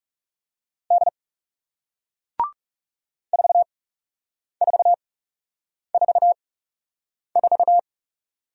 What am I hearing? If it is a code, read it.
Morse code, heard as a keyed tone, about 41 wpm, spaced as for 8 wpm, 700 Hz. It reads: D 4444